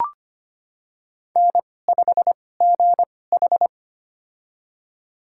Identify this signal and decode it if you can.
Morse code, heard as a keyed tone, about 25 wpm, spaced as for 18 wpm, 700 Hz. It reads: N5GH
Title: USA Call Signs - 4 Characters - 2x Character Spacing - 25wpm